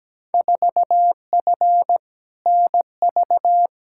Morse code, heard as a keyed tone, about 17 wpm, 700 Hz. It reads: 4F NV
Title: Field Day Exchanges Encore - 17wpm